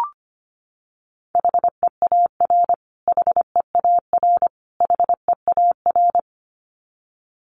Morse code, heard as a keyed tone, about 25 wpm, 700 Hz. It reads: HEAR HEAR HEAR
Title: Top 500 Words Repeated 25wpm